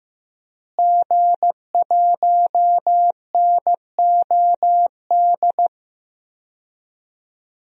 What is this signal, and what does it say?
Morse code, heard as a keyed tone, about 15 wpm, 700 Hz. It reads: G1NOD
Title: DXCC-100 Call Signs - 5 Characters - Repeated 2x - 15wpm